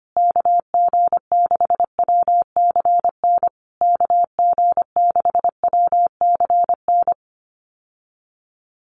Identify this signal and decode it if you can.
Morse code, heard as a keyed tone, about 25 wpm, 700 Hz. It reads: KG6WCN KG6WCN
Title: USA Call Signs Repeated 3x 25wpm